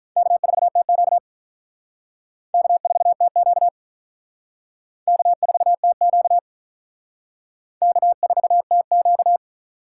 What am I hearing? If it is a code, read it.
Morse code, heard as a keyed tone, about 46 wpm, 700 Hz. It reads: K4TQ K4TQ K4TQ K4TQ